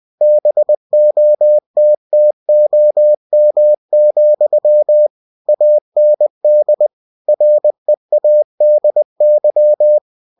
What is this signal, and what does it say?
Morse code, heard as a keyed tone, about 20 wpm, 600 Hz. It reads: BOTTOM, AND READY
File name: cw-books / winnie-the-pooh / winnie-20wpm-page-0000